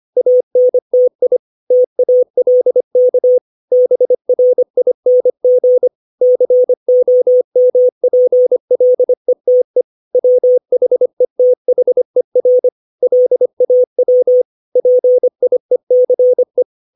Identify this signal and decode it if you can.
Morse code, heard as a keyed tone, about 25 wpm, 500 Hz. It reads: ANTI TALK BRING COMPLETE WHETHER LAW PIECE
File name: 632 words / 632 Words 25WPM